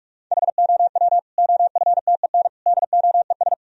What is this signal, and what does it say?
Morse code, heard as a keyed tone, about 41 wpm, 700 Hz. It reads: HOW OFTEN DOES
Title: Sentences from Top 1000 Words - Review Part 3 - 45wpm